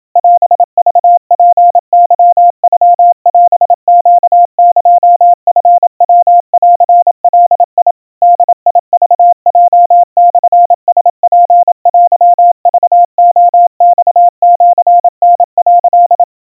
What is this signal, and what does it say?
Morse code (Japanese Wabun code), heard as a keyed tone, about 27 wpm, 700 Hz. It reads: オクツケノオネエチヤンガ ボクヲモラツテクレマシタ」
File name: kawa / Morse sample / kamyu 100 cpm 700 Hz Crispy